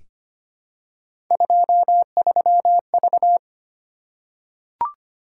Morse code, heard as a keyed tone, about 25 wpm, 700 Hz. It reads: 23V
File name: Three Letter-Number Combinations - 25wpm